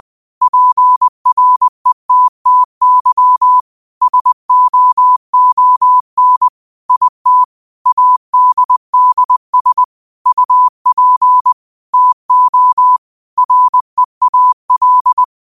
Morse code, heard as a keyed tone, about 20 wpm, 1000 Hz. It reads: PRETTY SOON IT ADDS UP TO REAL
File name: qotdfiles / 20wpm